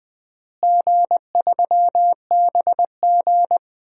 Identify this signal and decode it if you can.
Morse code, heard as a keyed tone, about 21 wpm, 700 Hz. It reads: G3BG
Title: DXCC-100 Call Signs - 4 Characters - Repeated 2x - 20wpm